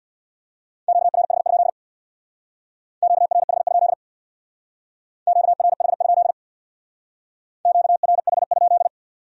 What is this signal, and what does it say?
Morse code, heard as a keyed tone, about 62 wpm, 700 Hz. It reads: QRH? QRH? QRH? QRH?
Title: Speed-Racing Q-Codes - 50wpm